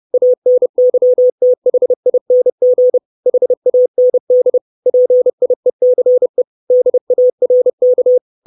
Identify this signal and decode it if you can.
Morse code, heard as a keyed tone, about 30 wpm, 500 Hz. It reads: ANYTHING HAND PIECE DARK
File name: 632 Words 30WPM